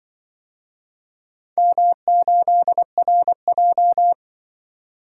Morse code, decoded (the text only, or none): M8RJ